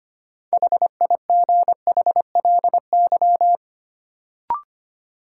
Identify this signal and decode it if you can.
Morse code, heard as a keyed tone, about 25 wpm, 700 Hz. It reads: HIGHLY
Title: Sentences from Top 1100 Words - New Word Drill Alt - 25wpm